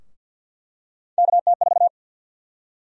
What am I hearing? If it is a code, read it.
Morse code, heard as a keyed tone, about 50 wpm, 700 Hz. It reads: KT4
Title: USA Call Sign Prefixes - Repeated 2x - 50wpm